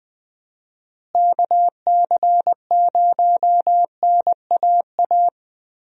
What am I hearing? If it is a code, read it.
Morse code, heard as a keyed tone, about 20 wpm, 700 Hz. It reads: KC0NAA
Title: USA Call Signs - 6 Characters - Repeated 3x - 20wpm